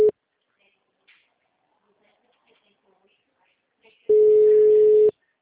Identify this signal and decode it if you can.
Ringback tone (Europe, 425 Hz), with speech.